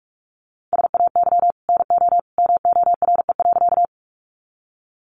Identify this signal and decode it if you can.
Morse code, heard as a keyed tone, about 44 wpm, 700 Hz. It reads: SAY NO MORE.
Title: Sentences from Top 200 Words 45wpm